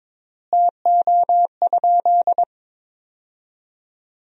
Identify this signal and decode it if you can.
Morse code, heard as a keyed tone, about 22 wpm, 700 Hz. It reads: TO?
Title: 3x Word-Spacing - Sentences from Top 500 Words - 22wpm